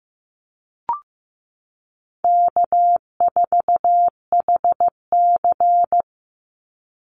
Morse code, heard as a keyed tone, about 15 wpm, 700 Hz. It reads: K4HC
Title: Easy Call Signs Repeated 3x - 15wpm